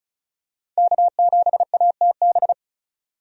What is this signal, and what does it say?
Morse code, heard as a keyed tone, about 36 wpm, 700 Hz. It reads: K7ATB
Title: USA Call Signs - 5 Characters - 35wpm